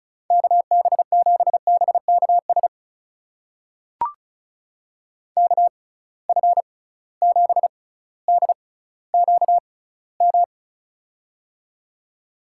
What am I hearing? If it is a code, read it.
Morse code, heard as a keyed tone, about 35 wpm, spaced as for 18 wpm, 700 Hz. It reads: KB7BKS KF7DQM
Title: USA Call Signs - 6 Characters - 6x Character Spacing - 35wpm